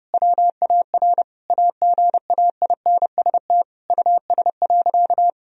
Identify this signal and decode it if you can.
Morse code, heard as a keyed tone, about 32 wpm, 700 Hz. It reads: WAR AGAINST US.
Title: Sentences from Top 1000 Words - Review Part 2 - 30wpm